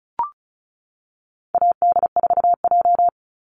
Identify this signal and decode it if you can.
Morse code, heard as a keyed tone, about 34 wpm, 700 Hz. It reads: AD4J
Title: Call Signs - 35wpm